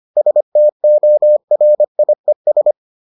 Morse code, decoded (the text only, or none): STORIES